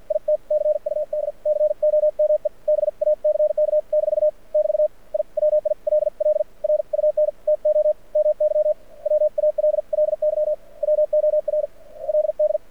Morse code, heard as a keyed tone, about 56 wpm, spaced as for 43 wpm, 590 Hz. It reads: IT CUD COME BACK- <BT> I WILL RUN TO MY WALLY WORLD
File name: archived / CRSnet